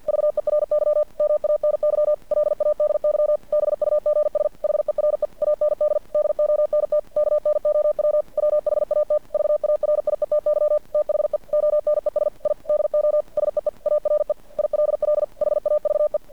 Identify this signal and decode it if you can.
Morse code, heard as a keyed tone, about 50 wpm, 600 Hz. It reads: VERY MANY LADY BUGS HERE AND DONT KNOW WHAT VARIETY THE ONES I DO SEE ARE ILL HAVE
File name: archived / CRSnet